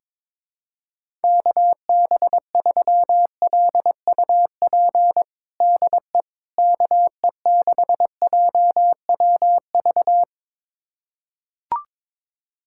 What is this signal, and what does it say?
Morse code, heard as a keyed tone, about 22 wpm, 700 Hz. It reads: KB3LUP DE KE6JWV
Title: USA Callsign DE Callsign 22wpm